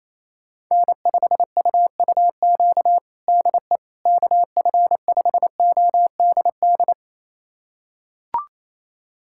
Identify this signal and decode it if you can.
Morse code, heard as a keyed tone, about 27 wpm, 700 Hz. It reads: N5UUQ DE KF5ODD
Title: USA Callsign DE Callsign 28wpm